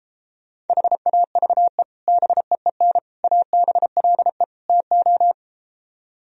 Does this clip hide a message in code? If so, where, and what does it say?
Yes: Morse code from 0.7 to 5.3 s, heard as a keyed tone, about 33 wpm, 700 Hz. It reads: HAVE BEEN ABLE TO